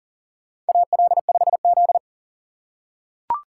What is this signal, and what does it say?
Morse code, heard as a keyed tone, about 40 wpm, 700 Hz. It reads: AL5Z
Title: USA Call Signs - 4 Characters - 4x Character Spacing - 40wpm